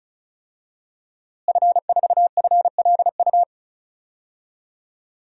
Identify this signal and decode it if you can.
Morse code, heard as a keyed tone, about 36 wpm, 700 Hz. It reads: F4FLU
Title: DXCC-100 Call Signs - 5 Characters - Repeated 3x - 35wpm